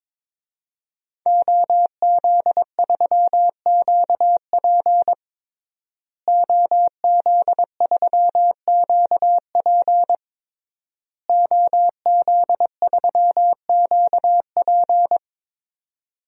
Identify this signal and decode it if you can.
Morse code, heard as a keyed tone, about 21 wpm, 700 Hz. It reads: OZ3QP OZ3QP OZ3QP
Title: DXCC-100 Call Signs - 5 Characters - Repeated 3x - 22wpm